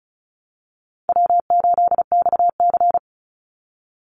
Morse code, heard as a keyed tone, about 34 wpm, 700 Hz. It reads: W8XC
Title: Easy Call Signs Repeated 2x - 35wpm